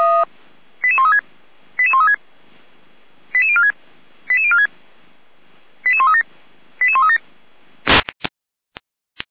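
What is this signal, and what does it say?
Five-tone selective call (ZVEI), heard as quick runs of five tones, tones of about 70 ms. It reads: 88136 88136 88046 88046 88137 88137
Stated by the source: ZVEI five-tone selective call 88190 received by SDR.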